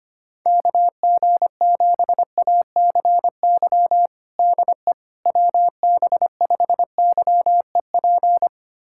Morse code, heard as a keyed tone, about 25 wpm, 700 Hz. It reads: KG7ACY DE WB5YEP
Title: USA Callsign DE Callsign - 6 Characters - 25wpm